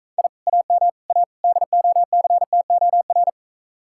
Morse code, heard as a keyed tone, about 43 wpm, 700 Hz. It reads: I AM A DOCTOR